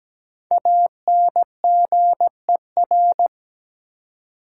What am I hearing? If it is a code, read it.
Morse code, heard as a keyed tone, about 17 wpm, 700 Hz. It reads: ANGER
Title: Five Letter Words - 17wpm